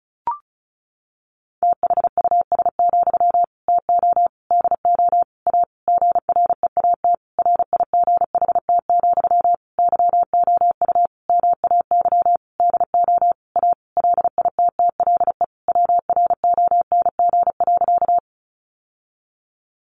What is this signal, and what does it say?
Morse code, heard as a keyed tone, about 34 wpm, 700 Hz. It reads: THUS, TO DO A GREAT RIGHT, YOU MAY DO A LITTLE WRONG.
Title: Sentences from Top 900 Words 35wpm